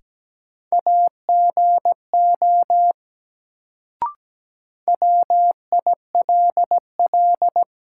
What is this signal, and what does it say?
Morse code, heard as a keyed tone, about 17 wpm, 700 Hz. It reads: AGO WILL
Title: Sets of 2 Words - New Word Drill - 17wpm